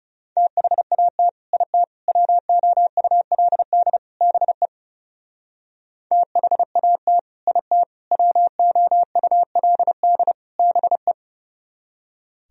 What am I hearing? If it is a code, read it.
Morse code, heard as a keyed tone, about 38 wpm, 700 Hz. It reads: THAT IT WOULD BE THAT IT WOULD BE